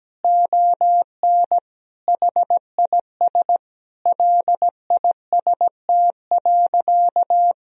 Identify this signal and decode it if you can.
Morse code, heard as a keyed tone, about 17 wpm, 700 Hz. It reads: ON HIS LIST.
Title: Sentences from Top 1000 Words - Review Part 2 - 17wpm